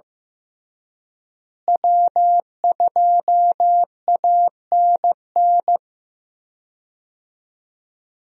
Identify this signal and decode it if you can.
Morse code, heard as a keyed tone, about 15 wpm, 700 Hz. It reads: W2ANN
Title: USA Call Signs - 5 Characters - Repeated 3x - 15wpm